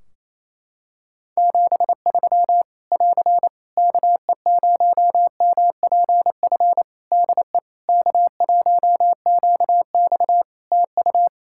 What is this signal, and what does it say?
Morse code, heard as a keyed tone, about 28 wpm, 700 Hz. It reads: 73 <AR> KE0MPF DE K1QX TU
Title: Practice QSO - Exchange 4 - Ending - 28wpm